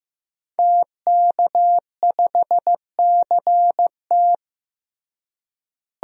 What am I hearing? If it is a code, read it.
Morse code, heard as a keyed tone, about 16 wpm, 700 Hz. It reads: TK5CT
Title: DXCC-100 Call Signs - 5 Characters - Repeated 2x - 15wpm